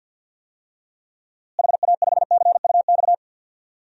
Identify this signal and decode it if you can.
Morse code, heard as a keyed tone, about 52 wpm, 700 Hz. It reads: HA5CUX
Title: DXCC-100 Call Signs - 6 Characters - Repeated 2x - 50wpm